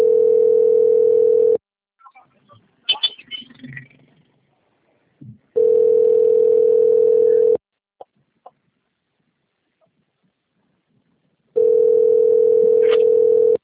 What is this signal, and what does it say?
Ringback tone (North America), 440 + 480 Hz.
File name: out- - - - - .47334